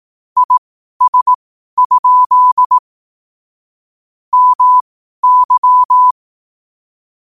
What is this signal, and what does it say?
Morse code, heard as a keyed tone, about 18 wpm, spaced as for 12 wpm, 1000 Hz. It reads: IS? MY
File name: qotdfiles / 12wpm